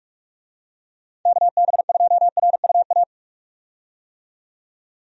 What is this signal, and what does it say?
Morse code, heard as a keyed tone, about 43 wpm, 700 Hz. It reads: KB2RUA
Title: USA Call Signs - 6 Characters - 45wpm